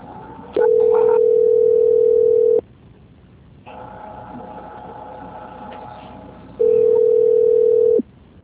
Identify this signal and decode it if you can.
Ringback tone (North America), 440 + 480 Hz.